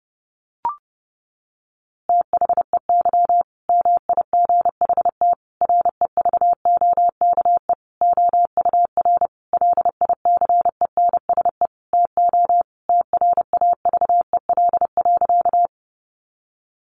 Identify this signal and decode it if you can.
Morse code, heard as a keyed tone, about 28 wpm, 700 Hz. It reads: THEY MIGHT REVOKE OUR LICENSE TO TRAVEL.
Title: A Sound Of Thunder 30wpm